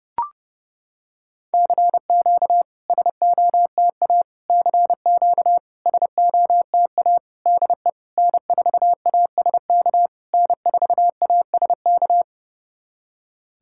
Morse code, heard as a keyed tone, about 30 wpm, 660 Hz. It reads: CQ SOTA CQ SOTA DE N4ASK N4ASK
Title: SOTA QSO Practice 30wpm